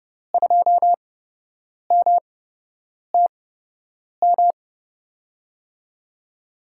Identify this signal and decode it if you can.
Morse code, heard as a keyed tone, about 30 wpm, spaced as for 8 wpm, 700 Hz. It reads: 2MTM
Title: USA Call Signs - 6 Characters - 8x Character Spacing - 30wpm